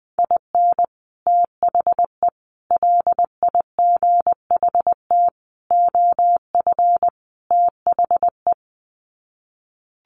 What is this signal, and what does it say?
Morse code, heard as a keyed tone, about 20 wpm, 700 Hz. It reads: IN THE LIGHT OF THE
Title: Sets of 5 Words Encore2 20wpm